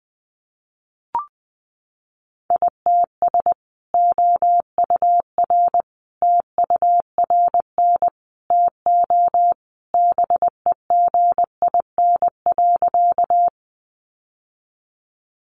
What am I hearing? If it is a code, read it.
Morse code, heard as a keyed tone, about 20 wpm, 700 Hz. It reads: ITS OUR TURN TO BEGIN.
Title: Sentences from Top 600 Words 20wpm